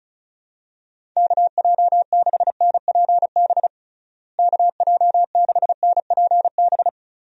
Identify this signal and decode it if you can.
Morse code, heard as a keyed tone, about 34 wpm, 700 Hz. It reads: KJ6NPB KJ6NPB
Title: USA Call Signs - 6 Characters - Repeated 3x - 35wpm